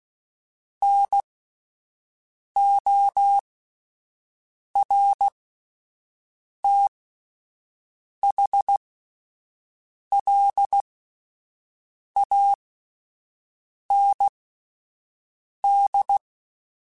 Morse code, heard as a keyed tone, about 16 wpm, spaced as for 5 wpm, 800 Hz. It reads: NORTHLAND